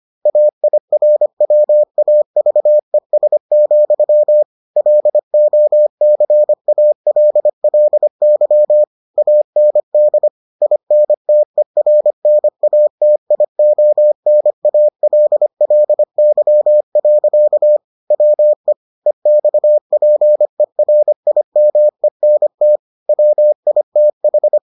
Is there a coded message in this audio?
Yes: Morse code from 0.3 to 24.6 s, heard as a keyed tone, about 25 wpm, 600 Hz. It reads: AIRWAVES, LOCALLY AND INTERNATIONALLY. WE EXPERIMENT WITH